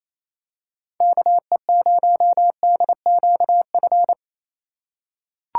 Morse code, heard as a keyed tone, about 27 wpm, 700 Hz. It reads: KE0DQF
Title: USA Call Signs - 6 Characters - 2x Character Spacing - 28wpm